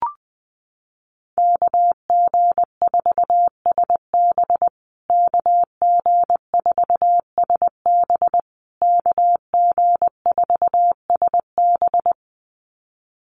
Morse code, heard as a keyed tone, about 20 wpm, 700 Hz. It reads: KG4SB KG4SB KG4SB
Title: USA Call Signs Repeated 3x 20wpm